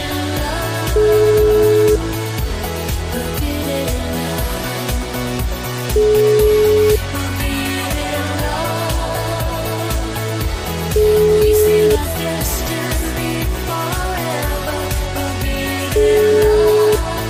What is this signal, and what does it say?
Ringback tone (Europe, 425 Hz), with noise.